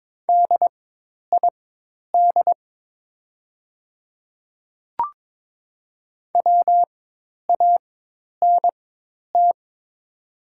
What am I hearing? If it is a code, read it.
Morse code, heard as a keyed tone, about 22 wpm, spaced as for 10 wpm, 700 Hz. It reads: DID WANT
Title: 4x Character Spacing - Top 200 Words - 22wpm